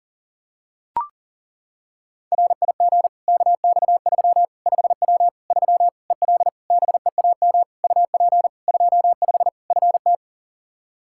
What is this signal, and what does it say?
Morse code, heard as a keyed tone, about 40 wpm, 700 Hz. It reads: RIG KX3 5W 3 EL BEAM UP 25 FT
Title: K1USN SST QSO Practice 40wpm